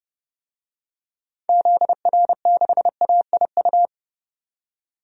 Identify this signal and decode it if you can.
Morse code, heard as a keyed tone, about 30 wpm, 700 Hz. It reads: ZR6AIU
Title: DXCC-100 Call Signs - 6 Characters - 30wpm